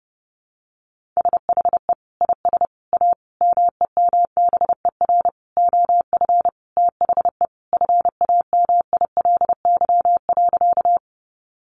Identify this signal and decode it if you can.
Morse code, heard as a keyed tone, about 28 wpm, 700 Hz. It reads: SHE IS A MEMBER OF THE FAMILY.
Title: Sentences from Top 1100 Words 30wpm